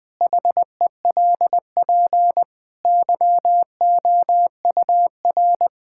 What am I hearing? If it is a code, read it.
Morse code, heard as a keyed tone, about 20 wpm, 700 Hz. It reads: HELP YOUR
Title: Sentences from Top 1000 Words - Review Part 5 - 20wpm